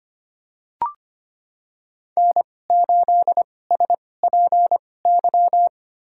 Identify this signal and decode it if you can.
Morse code, heard as a keyed tone, about 25 wpm, spaced as for 18 wpm, 700 Hz. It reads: N8SPY
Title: USA Call Signs - 5 Characters - 2x Character Spacing - 25wpm